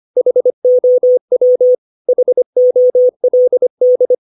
Morse code, heard as a keyed tone, about 25 wpm, 500 Hz. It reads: HOW HOLD
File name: 632 words / 632 Words 25WPM